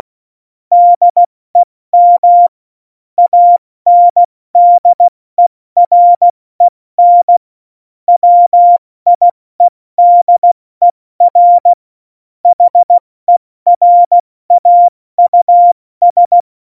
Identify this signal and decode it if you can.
Morse code, heard as a keyed tone, about 16 wpm, 700 Hz. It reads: DEM ANDEREN WIEDER HERAUS
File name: Morsekurs / Witze / Witz 13 March 2018, 70 F80